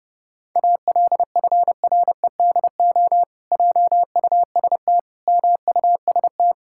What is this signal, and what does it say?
Morse code, heard as a keyed tone, about 30 wpm, 700 Hz. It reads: ALFREDO JUST MUST
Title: Pangram Challenge v2 - 30wpm